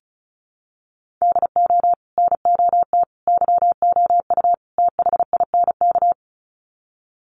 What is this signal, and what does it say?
Morse code, heard as a keyed tone, about 34 wpm, 700 Hz. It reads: DO NOT YOU THINK
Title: Sets of 4 Words 35wpm